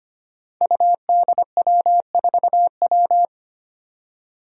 Morse code, heard as a keyed tone, about 25 wpm, 700 Hz. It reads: UDW4W
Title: Five Challenging Letter-Number Combinations 25wpm